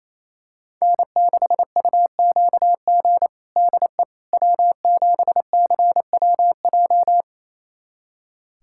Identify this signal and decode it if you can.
Morse code, heard as a keyed tone, about 28 wpm, 700 Hz. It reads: N6UQG DE W7CWJ
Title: USA Callsign DE Callsign - 5 Characters - 28wpm